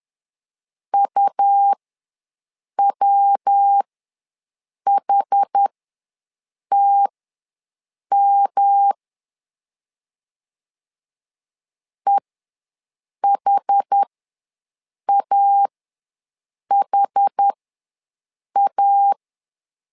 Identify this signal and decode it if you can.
Morse code, heard as a keyed tone, about 10 wpm, spaced as for 6 wpm, 800 Hz. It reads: UWHTM EHAHA
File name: LEZIONI / 3 MWRO / 3 MW ws 10wpm 2213